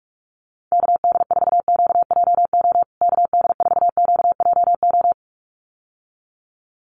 Morse code, heard as a keyed tone, about 44 wpm, 700 Hz. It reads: KD4QJO KD4QJO
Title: USA Call Signs Repeated 2x 45wpm